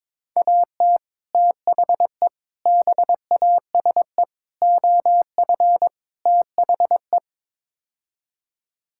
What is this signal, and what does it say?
Morse code, heard as a keyed tone, about 22 wpm, 700 Hz. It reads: AT THE BASE OF THE
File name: practice / 22wpm / Build A Phrase - 5-Word Phrases - 22wpm